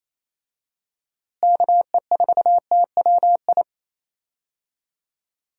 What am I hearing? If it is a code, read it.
Morse code, heard as a keyed tone, about 28 wpm, 700 Hz. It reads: KE4TWI
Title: USA Call Signs - 6 Characters - Repeated 3x - 28wpm